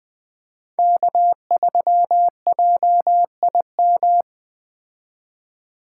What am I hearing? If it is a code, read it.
Morse code, heard as a keyed tone, about 20 wpm, 700 Hz. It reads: K3JIM
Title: Speed Racing - USA 1x3 Call Signs 17wpm